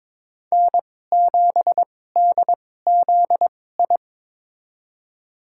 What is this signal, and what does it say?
Morse code, heard as a keyed tone, about 22 wpm, spaced as for 16 wpm, 700 Hz. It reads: N7DZI
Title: USA Call Signs - 5 Characters - 2x Character Spacing - 22wpm